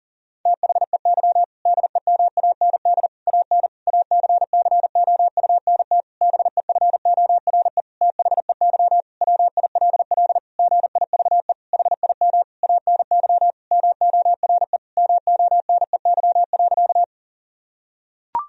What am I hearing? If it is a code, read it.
Morse code, heard as a keyed tone, about 40 wpm, 700 Hz. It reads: THEY DEMAND AN ACCOUNT BEFORE THEY WILL GIVE HIM ANY MORE MONEY.
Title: Sentences from Top 1000 Words - Review Part 4 - 40wpm